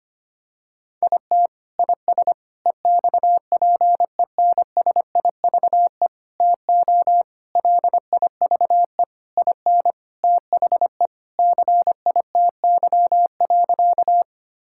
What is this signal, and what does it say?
Morse code, heard as a keyed tone, about 25 wpm, 700 Hz. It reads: IT IS EXPENSIVE TO LIVE IN THE CITY.
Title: Sentences from Top 1000 Words - Review Part 6 - 25wpm